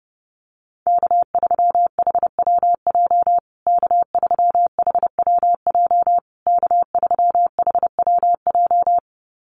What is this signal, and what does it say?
Morse code, heard as a keyed tone, about 30 wpm, 700 Hz. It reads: K3HWJ K3HWJ K3HWJ
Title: USA Call Signs Repeated 3x 30wpm